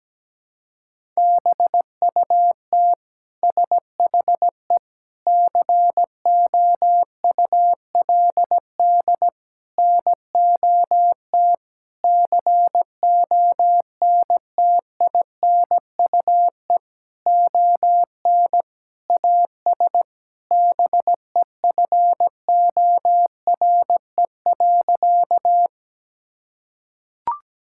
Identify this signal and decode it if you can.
Morse code, heard as a keyed tone, about 17 wpm, 700 Hz. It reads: BUT SHE COULD NOT CONTINUE ON AS BEFORE.
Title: Sentences from Top 1000 Words - Review Part 6 - 17wpm